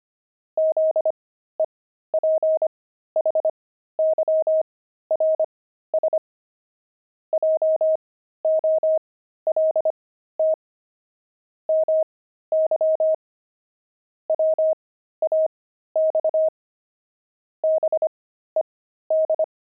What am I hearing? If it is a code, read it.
Morse code, heard as a keyed tone, about 25 wpm, spaced as for 13 wpm, 630 Hz. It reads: ZEPHYRS JOLT MY WAX BED
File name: pangram-fs-13-cs-25-five